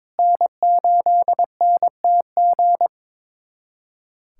Morse code, heard as a keyed tone, about 22 wpm, 700 Hz. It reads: N8NTG
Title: USA Callsign DE Callsign - 5 Characters - 22wpm